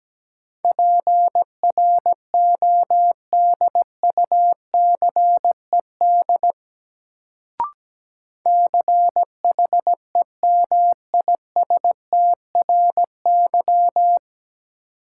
Morse code, heard as a keyed tone, about 17 wpm, 700 Hz. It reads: PRODUCED CHEMISTRY
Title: Sentences from Top 1000 Words - Review Part 6 - New Word Drill - 17wpm